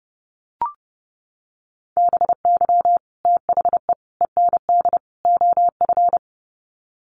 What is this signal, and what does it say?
Morse code, heard as a keyed tone, about 30 wpm, 700 Hz. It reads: BY THE END OF
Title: Sets of 4 Words 30wpm